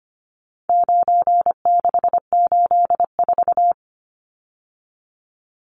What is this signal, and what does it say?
Morse code, heard as a keyed tone, about 25 wpm, 700 Hz. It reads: 9684
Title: Numbers - 4-Digit 25wpm